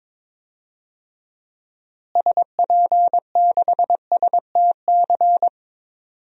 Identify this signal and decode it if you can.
Morse code, heard as a keyed tone, about 22 wpm, 700 Hz. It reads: SP6STC
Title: DXCC-100 Call Signs - 6 Characters - Repeated 3x - 22wpm